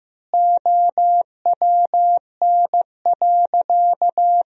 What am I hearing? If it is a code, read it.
Morse code, heard as a keyed tone, about 15 wpm, 700 Hz. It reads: OWN.
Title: 3x Word-Spacing - Sentences from Top 300 Words - 15wpm